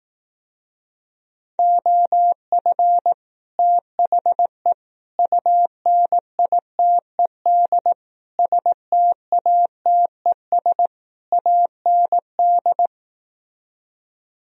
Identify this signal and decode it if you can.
Morse code, heard as a keyed tone, about 18 wpm, 700 Hz. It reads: OF THE UNITED STATES AND
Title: Speed-Racing - Sets of 5 Words 15wpm